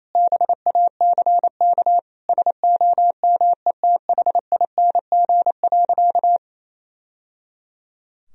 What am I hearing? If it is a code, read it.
Morse code, heard as a keyed tone, about 28 wpm, 700 Hz. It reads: BACK SOMETHING.
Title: Sentences from Top 1000 Words - Review Part 3 - 28wpm